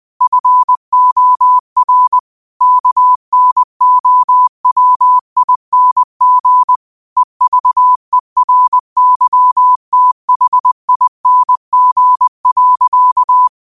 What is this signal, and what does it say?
Morse code, heard as a keyed tone, about 20 wpm, 1000 Hz. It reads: FOR KNOWING EVERYTHING.
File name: qotdfiles / 20wpm